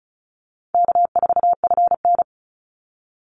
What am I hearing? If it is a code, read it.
Morse code, heard as a keyed tone, about 34 wpm, 700 Hz. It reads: K4FN
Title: Call Signs Repeated 2x - 35wpm